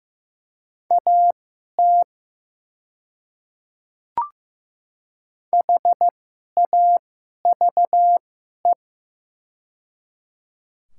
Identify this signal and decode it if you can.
Morse code, heard as a keyed tone, about 15 wpm, spaced as for 11 wpm, 700 Hz. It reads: AT HAVE
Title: 2x Character Spacing - Top 100 Words - 15wpm